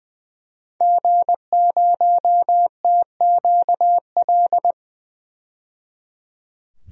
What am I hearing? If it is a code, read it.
Morse code, heard as a keyed tone, about 20 wpm, 700 Hz. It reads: G0TQL
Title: DXCC-100 Call Signs - 5 Characters - 20wpm